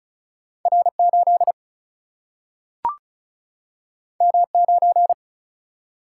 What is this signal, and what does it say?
Morse code, heard as a keyed tone, about 36 wpm, 700 Hz. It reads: R 8 M 9
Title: Single-Letter Single-Number 35wpm